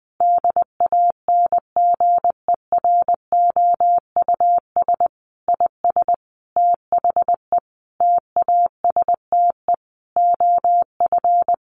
Morse code, heard as a keyed tone, about 20 wpm, 700 Hz. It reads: DANGEROUS IS THE TASTE OF
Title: Sentences from Top 1200 Words 20wpm